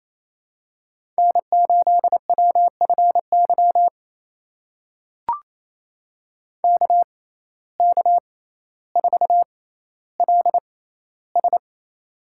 Morse code, heard as a keyed tone, about 28 wpm, spaced as for 14 wpm, 700 Hz. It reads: N8WFY KK4LS
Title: USA Call Signs - 5 Characters - 6x Character Spacing - 28wpm